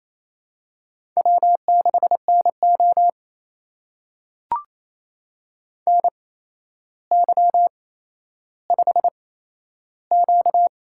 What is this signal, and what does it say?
Morse code, heard as a keyed tone, about 28 wpm, spaced as for 12 wpm, 700 Hz. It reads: W6NO NY5Q
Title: USA Call Signs - 4 Characters - 8x Character Spacing - 28wpm